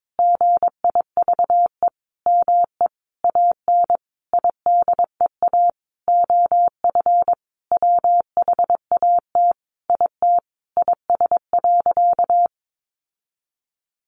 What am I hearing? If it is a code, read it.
Morse code, heard as a keyed tone, about 22 wpm, 700 Hz. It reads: GIVE ME AN IDEA OF WHAT IT IS.
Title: Sentences from Top 700 Words 22wpm